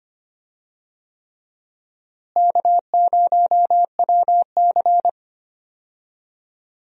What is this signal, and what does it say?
Morse code, heard as a keyed tone, about 25 wpm, 700 Hz. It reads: K0WC
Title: USA Call Signs - 4 Characters - Repeated 3x - 25wpm